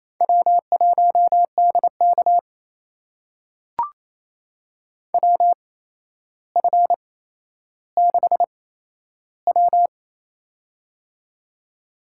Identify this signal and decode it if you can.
Morse code, heard as a keyed tone, about 28 wpm, spaced as for 12 wpm, 700 Hz. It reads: W1DK WF6W
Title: USA Call Signs - 4 Characters - 8x Character Spacing - 28wpm